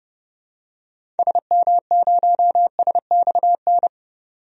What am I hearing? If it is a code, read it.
Morse code, heard as a keyed tone, about 32 wpm, 700 Hz. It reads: SM0SXN
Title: DXCC-100 Call Signs - 6 Characters - 30wpm